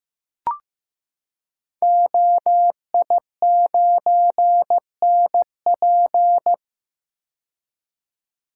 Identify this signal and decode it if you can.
Morse code, heard as a keyed tone, about 15 wpm, 700 Hz. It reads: OI9NP
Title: Code Groups - 5 Letter:Numbers - 15wpm